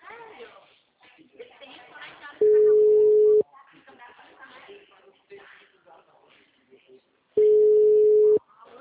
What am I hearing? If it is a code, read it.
Ringback tone (Europe, 425 Hz), with speech.